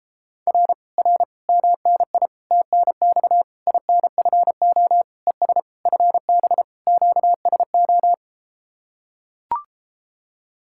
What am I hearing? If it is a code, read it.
Morse code, heard as a keyed tone, about 33 wpm, 700 Hz. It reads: R R MNI TNX INFO ES FB QSO